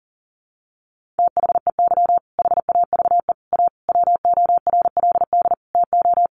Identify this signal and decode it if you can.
Morse code, heard as a keyed tone, about 40 wpm, 700 Hz. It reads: THEY HAVE A WORLD TO
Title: Sentences from Top 1500 Words 40wpm